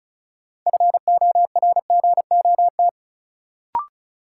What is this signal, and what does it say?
Morse code, heard as a keyed tone, about 34 wpm, 700 Hz. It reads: FORGOT
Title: Sentences from Top 1400 Words - New Word Drill Alt - 35wpm